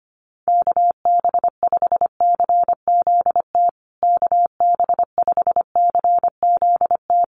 Morse code, heard as a keyed tone, about 25 wpm, 700 Hz. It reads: KB5CZT KB5CZT
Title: USA Call Signs Repeated 2x 25wpm